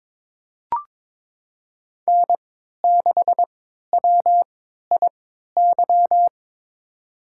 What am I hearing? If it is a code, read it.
Morse code, heard as a keyed tone, about 21 wpm, spaced as for 12 wpm, 700 Hz. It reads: N6WIY
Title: USA Call Signs - 5 Characters - 3x Character Spacing - 22wpm